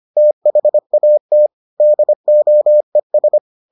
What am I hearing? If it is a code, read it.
Morse code, heard as a keyed tone, about 25 wpm, 600 Hz. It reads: THAT DOES